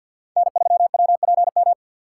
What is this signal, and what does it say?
Morse code, heard as a keyed tone, about 52 wpm, 700 Hz. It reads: N3WPM
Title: USA Call Signs - 5 Characters - 3x Character Spacing - 50wpm